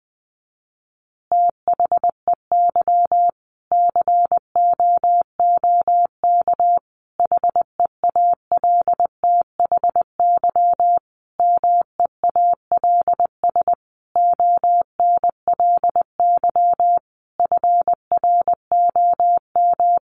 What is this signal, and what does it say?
Morse code, heard as a keyed tone, about 20 wpm, 700 Hz. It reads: THEY COOK HEALTHY MEALS ONLY FROM
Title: Easy Phrases 20wpm